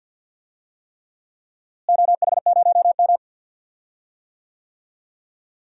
Morse code, heard as a keyed tone, about 46 wpm, 700 Hz. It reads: OH0M
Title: DXCC-100 Call Signs - 4 Characters - Repeated 3x - 50wpm